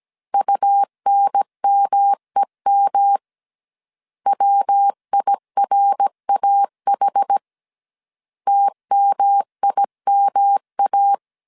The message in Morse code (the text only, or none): UNMEM WIRAH TMIMA